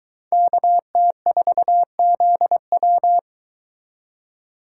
Morse code, heard as a keyed tone, about 23 wpm, 700 Hz. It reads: KT4ZW